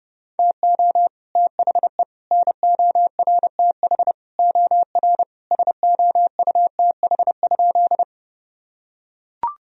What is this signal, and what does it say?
Morse code, heard as a keyed tone, about 30 wpm, 700 Hz. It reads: TO THE NORTH OR SOUTH?
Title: Sentences from Top 1000 Words - Review Part 2 - 30wpm